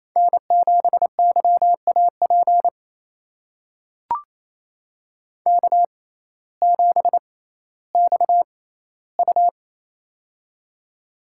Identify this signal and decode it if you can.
Morse code, heard as a keyed tone, about 28 wpm, spaced as for 15 wpm, 700 Hz. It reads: N7YAP K7XU
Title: USA Call Signs - 6x Character Spacing - 28wpm